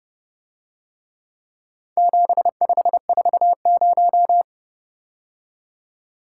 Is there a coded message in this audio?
Yes: Morse code from 2.0 to 4.4 s, heard as a keyed tone, about 29 wpm, 700 Hz. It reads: 7540